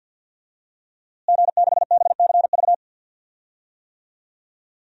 Morse code, heard as a keyed tone, about 52 wpm, 700 Hz. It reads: G6BCV
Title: DXCC-100 Call Signs - 5 Characters - Repeated 3x - 50wpm